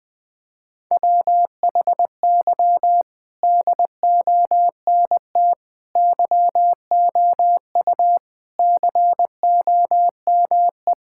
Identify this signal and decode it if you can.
Morse code, heard as a keyed tone, about 20 wpm, 700 Hz. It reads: WHY DONT YOU COME
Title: Sentences with 4-Word Phrases - 20wpm